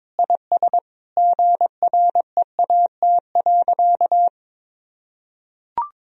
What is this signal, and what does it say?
Morse code, heard as a keyed tone, about 22 wpm, 700 Hz. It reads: IS GREAT.
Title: Sentences from Top 1000 Words - Review Part 4 - 22wpm